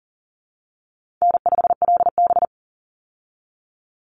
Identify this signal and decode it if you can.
Morse code, heard as a keyed tone, about 40 wpm, 700 Hz. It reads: N5LB
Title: Easy Call Signs Repeated 3x - 40wpm